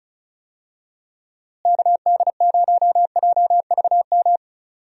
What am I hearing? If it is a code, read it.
Morse code, heard as a keyed tone, about 36 wpm, 700 Hz. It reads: KD0JVM
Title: USA Call Signs - 6 Characters - Repeated 2x - 35wpm